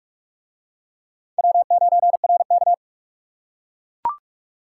Morse code, heard as a keyed tone, about 43 wpm, 700 Hz. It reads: W9RK